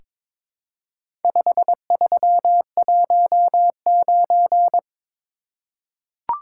Morse code, heard as a keyed tone, about 22 wpm, 700 Hz. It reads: 5319
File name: practice / 22wpm / Four Letter-Number Combinations - 22wpm